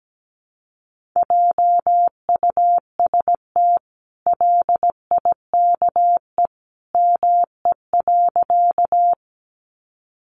Morse code, heard as a keyed tone, about 17 wpm, 700 Hz. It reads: JUST LIKE ME.
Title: Sentences from Top 200 Words 17wpm